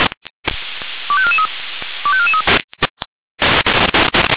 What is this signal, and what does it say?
Five-tone selective call (ZVEI), heard as quick runs of five tones, tones of about 80 ms. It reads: 25443 25443
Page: SDR/ZVEI